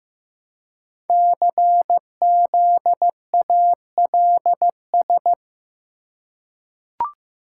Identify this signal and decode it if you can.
Morse code, heard as a keyed tone, about 15 wpm, 700 Hz. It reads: CZALS
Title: Code Groups - 5 Letters - 15wpm